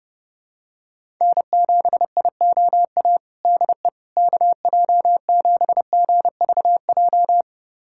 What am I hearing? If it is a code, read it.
Morse code, heard as a keyed tone, about 30 wpm, 700 Hz. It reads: N7IOA DE KJ7GVJ
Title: USA Callsign DE Callsign 30wpm